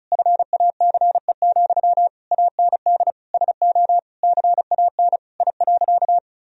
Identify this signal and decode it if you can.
Morse code, heard as a keyed tone, about 34 wpm, 700 Hz. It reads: FACE, AND SO CAN I.
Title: Sentences from Top 1000 Words - Review Part 2 - 35wpm